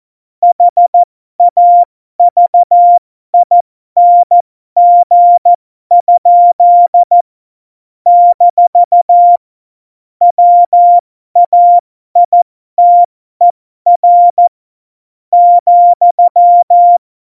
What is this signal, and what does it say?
Morse code, heard as a keyed tone, about 14 wpm, spaced as for 12 wpm, 700 Hz. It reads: HAVING? - WAITER ,
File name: Morsekurs / Witze